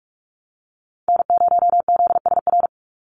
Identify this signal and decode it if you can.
Morse code, heard as a keyed tone, about 44 wpm, 700 Hz. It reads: N0ZSR
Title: USA Call Signs 45wpm